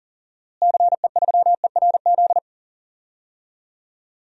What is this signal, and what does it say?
Morse code, heard as a keyed tone, about 40 wpm, 700 Hz. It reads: CE3ERZ
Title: DXCC-100 Call Signs - 6 Characters - Repeated 3x - 40wpm